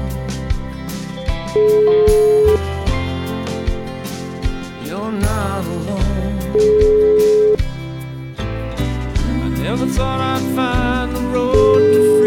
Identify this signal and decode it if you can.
Ringback tone (Europe, 425 Hz), with noise.